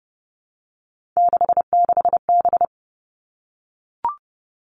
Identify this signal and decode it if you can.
Morse code, heard as a keyed tone, about 30 wpm, 700 Hz. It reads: 66B
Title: All Cut Numbers - 3-Digit 30wpm